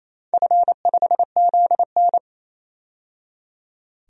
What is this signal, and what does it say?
Morse code, heard as a keyed tone, about 28 wpm, 700 Hz. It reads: F5ZN
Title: DXCC-100 Call Signs - 4 Characters - 28wpm